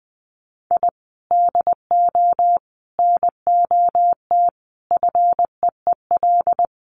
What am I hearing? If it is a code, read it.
Morse code, heard as a keyed tone, about 19 wpm, 700 Hz. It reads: I DO NOT FEEL
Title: Sets of 5 Words Encore 20wpm